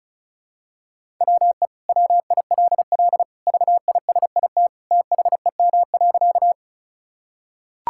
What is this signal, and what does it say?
Morse code, heard as a keyed tone, about 36 wpm, 700 Hz. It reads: WE WILL VISIT THEM.
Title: Sentences from Top 1000 Words - Review Part 6 - 35wpm